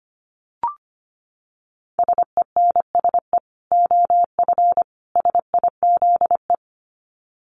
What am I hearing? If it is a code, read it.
Morse code, heard as a keyed tone, about 25 wpm, 700 Hz. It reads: SENSE OF SIZE
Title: Dit Challenge 25wpm